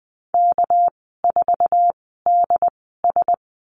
Morse code, heard as a keyed tone, about 20 wpm, spaced as for 14 wpm, 700 Hz. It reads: K4DS
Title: 213 Lesson 34g - Number 8 - Call Signs 20wpm